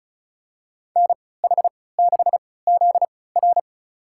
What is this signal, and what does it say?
Morse code, heard as a keyed tone, about 36 wpm, spaced as for 20 wpm, 700 Hz. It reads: NH6ZR
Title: USA Call Signs - 5 Characters - 3x Character Spacing - 35wpm